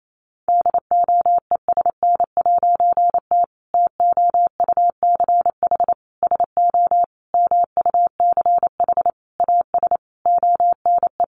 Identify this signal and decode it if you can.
Morse code, heard as a keyed tone, about 28 wpm, 700 Hz. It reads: DOESN'T TOUCH SO MUCH AS ONE
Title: A Sound Of Thunder 28wpm